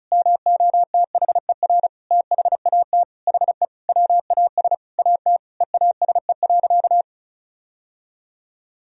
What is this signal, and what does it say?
Morse code, heard as a keyed tone, about 34 wpm, 700 Hz. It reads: MOTHER THAT HE WAS AT EASE.
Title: Sentences from Top 1000 Words - Review Part 1 - 35wpm